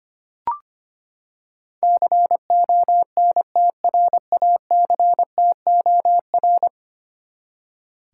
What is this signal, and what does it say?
Morse code, heard as a keyed tone, about 25 wpm, 700 Hz. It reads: CONTRACTOR
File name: Ten Letter Words - 25wpm